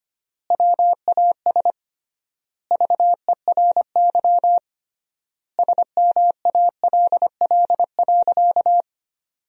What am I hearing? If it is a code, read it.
Morse code, heard as a keyed tone, about 25 wpm, 700 Hz. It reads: WAS VERY SMALL.
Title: 3x Word-Spacing - Sentences from Top 400 Words - 25wpm